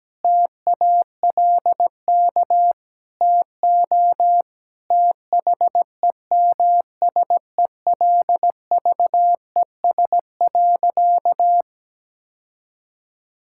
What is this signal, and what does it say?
Morse code, heard as a keyed tone, about 17 wpm, 700 Hz. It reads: TALK TO THEMSELVES.
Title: Sentences from Top 1000 Words - Review Part 5 - 17wpm